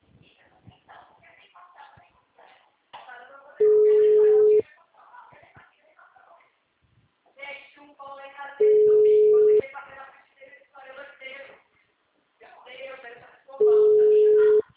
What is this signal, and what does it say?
Ringback tone (Europe, 425 Hz), with speech.